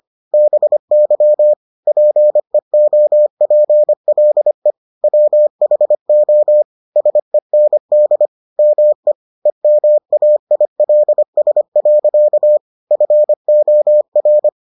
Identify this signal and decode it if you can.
Morse code, heard as a keyed tone, about 25 wpm, 600 Hz. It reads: BY PEOPLE WHO SEND ME EMAILS. FOR